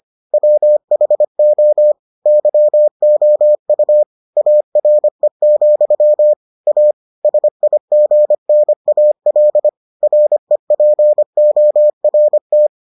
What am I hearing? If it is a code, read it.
Morse code, heard as a keyed tone, about 25 wpm, 600 Hz. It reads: WHO YOU ARE, A SIGNAL REPORT